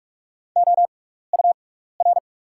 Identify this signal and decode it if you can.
Morse code, heard as a keyed tone, about 44 wpm, spaced as for 15 wpm, 700 Hz. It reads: OUR
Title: 6x Character Spacing - Top 300 Words - 45wpm